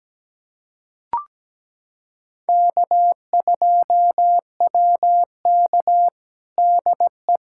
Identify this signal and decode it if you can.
Morse code, heard as a keyed tone, about 17 wpm, 700 Hz. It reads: K2WK DE
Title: USA Callsign DE Callsign - 4 Characters - 17wpm